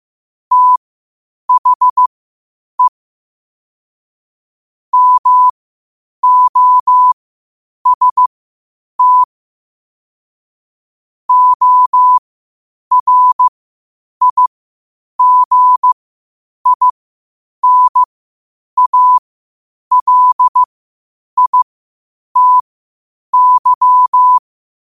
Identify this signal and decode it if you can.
Morse code, heard as a keyed tone, about 15 wpm, spaced as for 8 wpm, 1000 Hz. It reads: THE MOST ORIGINALITY
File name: qotdfiles / 8wpm